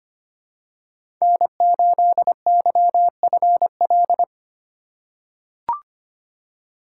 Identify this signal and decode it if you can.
Morse code, heard as a keyed tone, about 25 wpm, 700 Hz. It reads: N8YFL
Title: USA Call Signs - 5 Characters - 2x Character Spacing - 25wpm